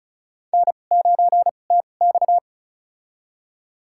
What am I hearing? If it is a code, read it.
Morse code, heard as a keyed tone, about 36 wpm, spaced as for 25 wpm, 700 Hz. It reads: N9TX
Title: USA Call Signs - 4 Characters - 2x Character Spacing - 35wpm